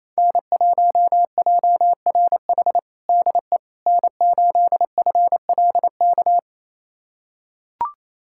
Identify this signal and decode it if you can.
Morse code, heard as a keyed tone, about 28 wpm, 700 Hz. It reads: N1JRH DE N8FLK
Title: USA Callsign DE Callsign - 5 Characters - 28wpm